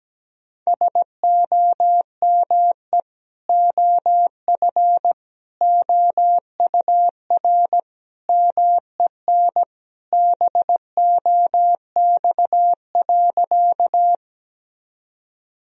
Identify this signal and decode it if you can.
Morse code, heard as a keyed tone, about 17 wpm, 700 Hz. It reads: SOME OF OUR MEN BOX.
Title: Sentences from Top 1000 Words - Review Part 2 - 17wpm